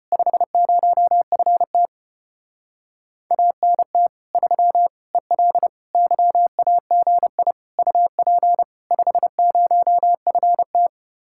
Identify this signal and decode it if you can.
Morse code, heard as a keyed tone, about 33 wpm, 700 Hz. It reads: 50FT ANT 3 EL YAGI UP 50FT